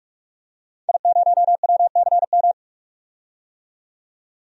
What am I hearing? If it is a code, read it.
Morse code, heard as a keyed tone, about 45 wpm, 700 Hz. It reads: I0WCM
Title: DXCC-100 Call Signs - 5 Characters - 45wpm